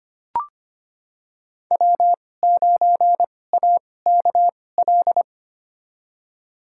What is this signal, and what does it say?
Morse code, heard as a keyed tone, about 25 wpm, spaced as for 18 wpm, 700 Hz. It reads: W9AKL
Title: USA Call Signs - 2x Character Spacing - 25wpm